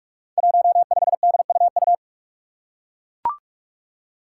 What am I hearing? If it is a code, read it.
Morse code, heard as a keyed tone, about 41 wpm, 700 Hz. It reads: 15DUU
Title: Five Challenging Letter-Number Combinations 45wpm